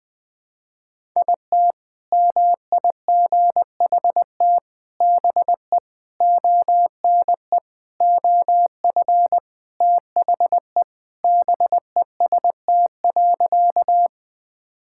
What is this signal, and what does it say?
Morse code, heard as a keyed tone, about 20 wpm, 700 Hz. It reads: IT MIGHT BE ONE OF THE BEST.
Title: Sentences with 4-Word Phrases - 20wpm